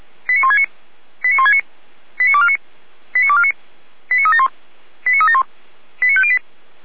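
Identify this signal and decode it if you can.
Five-tone selective call (ZVEI), heard as quick runs of five tones, tones of about 70 ms. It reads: 89169 89169 89249 89249 89371 89371 89508
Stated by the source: ZVEI five-tone selective call received by SDR.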